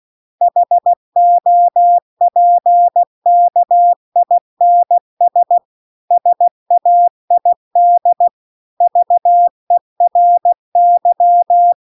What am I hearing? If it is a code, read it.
Morse code, heard as a keyed tone, about 16 wpm, 700 Hz. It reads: HOPKINS SAID VERY